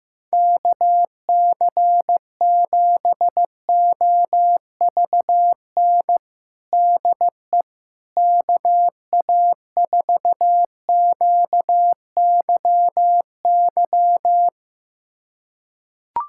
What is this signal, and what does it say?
Morse code, heard as a keyed tone, about 15 wpm, 700 Hz. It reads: KC7OVN DE KA4QYY
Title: USA Callsign DE Callsign 15wpm